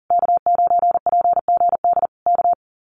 Morse code, heard as a keyed tone, about 40 wpm, 700 Hz. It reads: K9PGD K
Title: Common QSO Phrases Encore 40wpm